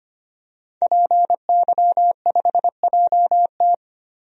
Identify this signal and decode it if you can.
Morse code, heard as a keyed tone, about 25 wpm, 700 Hz. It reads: PY5JT
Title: DXCC-100 Call Signs - 5 Characters - 25wpm